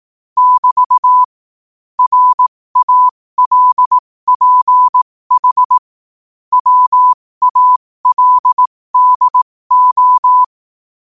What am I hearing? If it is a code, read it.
Morse code, heard as a keyed tone, about 18 wpm, spaced as for 15 wpm, 1000 Hz. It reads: <BT> RALPH WALDO
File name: qotdfiles / 15wpm